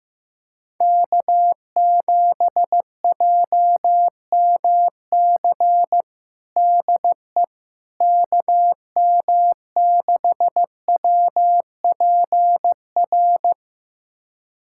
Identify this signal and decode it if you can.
Morse code, heard as a keyed tone, about 15 wpm, 700 Hz. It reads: K7JMC DE KM6WPR
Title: USA Callsign DE Callsign 15wpm